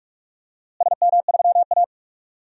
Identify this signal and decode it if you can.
Morse code, heard as a keyed tone, about 41 wpm, 700 Hz. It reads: SM3A
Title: DXCC-100 Call Signs - 4 Characters - Repeated 3x - 45wpm